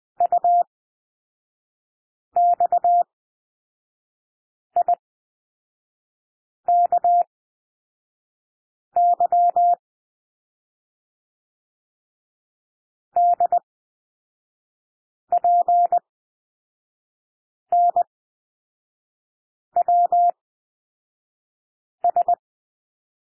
Morse code, heard as a keyed tone, about 20 wpm, spaced as for 5 wpm, 700 Hz. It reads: UXIKY DPNWS
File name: SOUND LETTERS 20 wpm